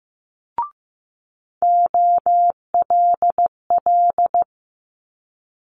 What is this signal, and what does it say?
Morse code, heard as a keyed tone, about 15 wpm, 700 Hz. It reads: OLL
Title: Multi-letter Phonemes Quick Fire 15wpm